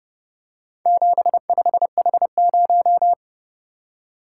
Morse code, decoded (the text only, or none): 75H0